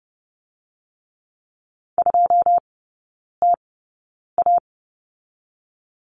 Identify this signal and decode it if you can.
Morse code, heard as a keyed tone, about 30 wpm, spaced as for 9 wpm, 700 Hz. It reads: 2TA
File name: Lesson 18e - Letter m - Three Characters 30wpm